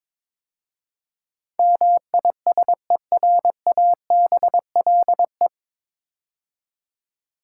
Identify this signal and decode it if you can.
Morse code, heard as a keyed tone, about 22 wpm, 700 Hz. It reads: MISERABLE
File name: 9 Letter Words - First 3 Letters Given - 22wpm